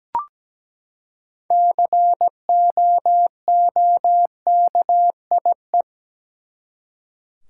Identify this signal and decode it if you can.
Morse code, heard as a keyed tone, about 17 wpm, 700 Hz. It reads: COOKIE
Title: Six Letter Words - 17wpm